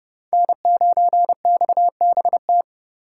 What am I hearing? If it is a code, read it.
Morse code, heard as a keyed tone, about 29 wpm, 700 Hz. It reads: N9XBT